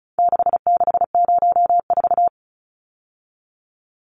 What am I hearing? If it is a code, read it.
Morse code, heard as a keyed tone, about 34 wpm, 700 Hz. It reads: 6604
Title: Numbers - 4-Digit 35wpm